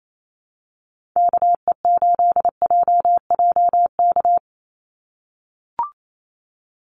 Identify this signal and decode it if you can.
Morse code, heard as a keyed tone, about 28 wpm, 700 Hz. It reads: KE8JJK
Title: USA Call Signs 28wpm